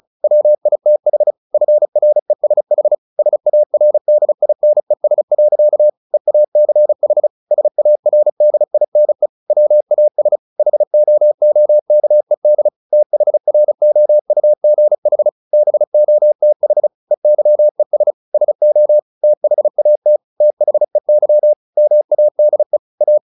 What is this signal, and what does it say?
Morse code, heard as a keyed tone, about 34 wpm, 600 Hz. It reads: WITH FRESH SARDINES. EACH SARDINE WAS HOOKED THROUGH BOTH EYES SO THAT THEY MADE A
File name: the old man and the sea / the old man and the sea 1 35wpm